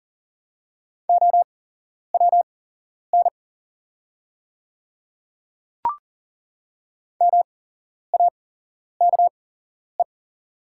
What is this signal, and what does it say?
Morse code, heard as a keyed tone, about 40 wpm, spaced as for 11 wpm, 690 Hz. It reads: OWN MAKE
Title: 8x Character Spacing - Top 300 Words - 40wpm